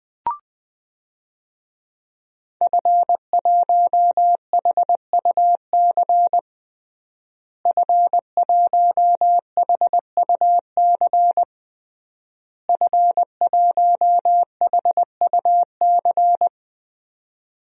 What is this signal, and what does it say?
Morse code, heard as a keyed tone, about 21 wpm, 700 Hz. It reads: F1HUC F1HUC F1HUC
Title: DXCC-100 Call Signs - 5 Characters - Repeated 3x - 20wpm